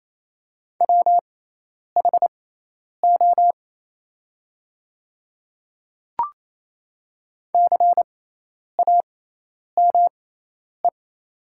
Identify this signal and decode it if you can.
Morse code, heard as a keyed tone, about 28 wpm, spaced as for 10 wpm, 690 Hz. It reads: WHO CAME
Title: 6x Character Spacing - Top 300 Words - 28wpm